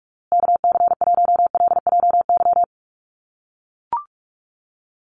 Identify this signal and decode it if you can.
Morse code, heard as a keyed tone, about 44 wpm, 700 Hz. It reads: KC1LJY
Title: USA Call Signs Repeated 3x 45wpm